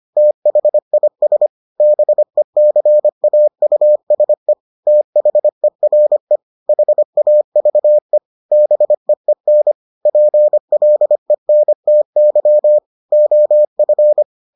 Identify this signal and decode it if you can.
Morse code, heard as a keyed tone, about 25 wpm, 600 Hz. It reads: THIS BECAUSE THERE HAVE BEEN PLENTY OF